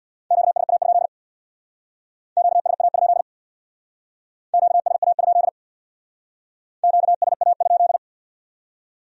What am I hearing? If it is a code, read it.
Morse code, heard as a keyed tone, about 62 wpm, 700 Hz. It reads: QSA? QSA? QSA? QSA?
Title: Speed-Racing Q-Codes - 50wpm